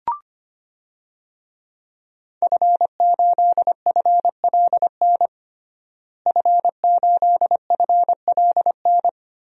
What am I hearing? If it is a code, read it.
Morse code, heard as a keyed tone, about 25 wpm, 700 Hz. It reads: F8FLN F8FLN
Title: DXCC-100 Call Signs - 5 Characters - Repeated 2x - 25wpm